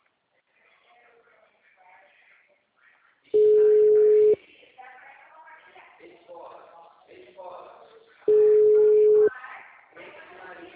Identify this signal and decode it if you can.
Ringback tone (Europe, 425 Hz), with speech.
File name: out- -2004- - - .24